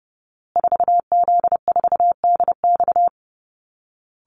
Morse code, heard as a keyed tone, about 30 wpm, 700 Hz. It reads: 4Z4DX
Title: Call Signs Repeated 2x - 30wpm